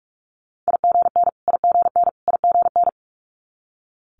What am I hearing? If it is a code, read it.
Morse code, heard as a keyed tone, about 46 wpm, 700 Hz. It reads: IGN IGN IGN
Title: Multi-letter Phonemes Repeated 3x 45wpm